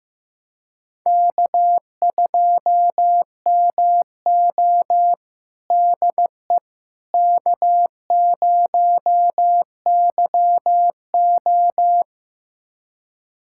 Morse code, heard as a keyed tone, about 15 wpm, 700 Hz. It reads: K2MO DE K0YO
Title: USA Callsign DE Callsign - 4 Characters - 15wpm